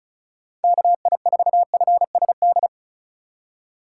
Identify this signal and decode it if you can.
Morse code, heard as a keyed tone, about 36 wpm, 700 Hz. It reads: KI4FSD
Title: USA Call Signs - 6 Characters - 4x Character Spacing - 35wpm